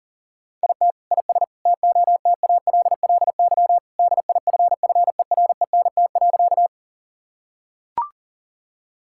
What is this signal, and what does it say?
Morse code, heard as a keyed tone, about 40 wpm, 700 Hz. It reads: IT IS TOTALLY DIFFERENT.
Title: Sentences from Top 1600 Words 40wpm